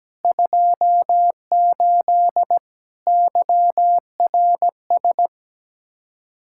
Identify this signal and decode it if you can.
Morse code, heard as a keyed tone, about 17 wpm, 700 Hz. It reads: 28 YRS
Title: K1USN SST QSO Practice 17wpm